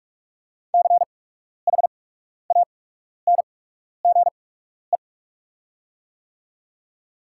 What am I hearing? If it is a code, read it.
Morse code, heard as a keyed tone, about 48 wpm, spaced as for 12 wpm, 690 Hz. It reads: CHANGE
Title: 8x Character Spacing - Top 200 Words - 45wpm